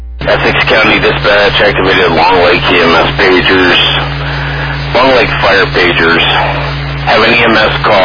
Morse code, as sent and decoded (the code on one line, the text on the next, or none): none